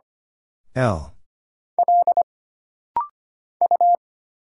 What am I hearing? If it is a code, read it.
Morse code, heard as a keyed tone, about 25 wpm, 700 Hz. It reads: E L U
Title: Single Letter-Number Alt - Rapid-Fire 25wpm